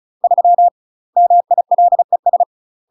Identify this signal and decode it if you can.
Morse code, heard as a keyed tone, about 34 wpm, 700 Hz. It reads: 3 MILES
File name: UK MASTER QSOs TRAINING / 35WPM / UK MASTER QSOs 0000